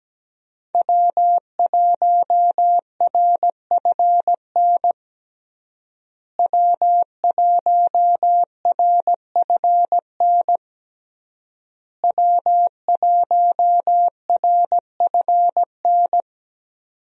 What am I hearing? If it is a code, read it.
Morse code, heard as a keyed tone, about 17 wpm, 700 Hz. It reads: W1RFN W1RFN W1RFN
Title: USA Call Signs - 5 Characters - Repeated 3x - 17wpm